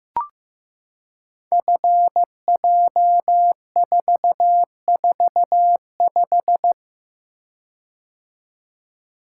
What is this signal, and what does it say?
Morse code, heard as a keyed tone, about 15 wpm, 700 Hz. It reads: FJ445
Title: Code Groups - 5 Letter:Numbers - 15wpm